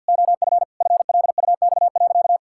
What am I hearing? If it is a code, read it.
Morse code, heard as a keyed tone, about 50 wpm, 700 Hz. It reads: OF FLUX.
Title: Pangram Challenge v2 - 50wpm